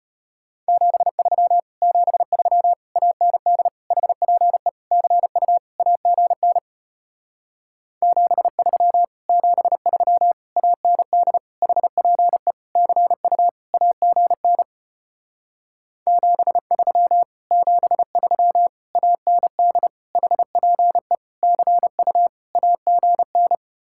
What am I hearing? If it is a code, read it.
Morse code, heard as a keyed tone, about 33 wpm, 700 Hz. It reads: 73 73 AND HPE CU AGN 73 73 AND HPE CU AGN 73 73 AND HPE CU AGN